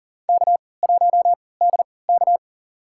Morse code, heard as a keyed tone, about 39 wpm, spaced as for 22 wpm, 700 Hz. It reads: K1DK
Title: USA Call Signs - 3x Character Spacing - 40wpm